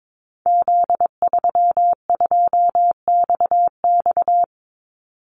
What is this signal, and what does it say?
Morse code, heard as a keyed tone, about 22 wpm, 700 Hz. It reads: Z32XX
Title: Call Signs Repeated 2x - 22wpm